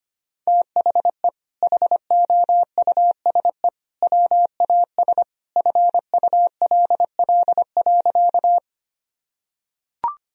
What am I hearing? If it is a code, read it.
Morse code, heard as a keyed tone, about 25 wpm, 700 Hz. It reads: THE HOUSE WAS FULL.
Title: Sentences from Top 1000 Words - Review Part 3 - 25wpm